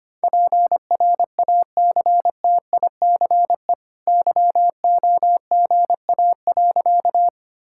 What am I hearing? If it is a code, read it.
Morse code, heard as a keyed tone, about 25 wpm, 690 Hz. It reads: PRACTICE YOGA.
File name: Easy Sentences with Hints - 25wpm